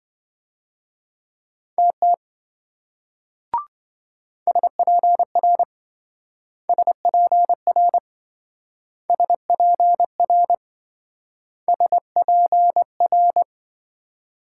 Morse code, heard as a keyed tone, about 26 wpm, 700 Hz. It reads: TT SPR SPR SPR SPR